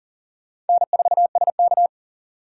Morse code, heard as a keyed tone, about 54 wpm, spaced as for 47 wpm, 700 Hz. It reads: N4SK
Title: Speed Racing - USA 1x2 Call Signs 40wpm